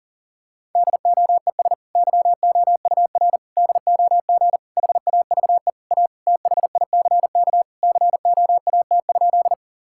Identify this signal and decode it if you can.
Morse code, heard as a keyed tone, about 40 wpm, 700 Hz. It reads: DOES YOUR DOG HAVE A THICK COAT?
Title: Sentences from Top 1600 Words 40wpm